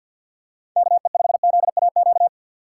Morse code, heard as a keyed tone, about 52 wpm, 700 Hz. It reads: KE5ZAQ
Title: USA Call Signs - 6 Characters - 6x Character Spacing - 50wpm